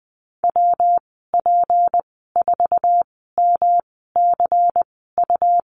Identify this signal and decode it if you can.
Morse code, heard as a keyed tone, about 19 wpm, 700 Hz. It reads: W P 4 M C U
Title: 231 Lesson 37g - Letter x - Call Signs 20wpm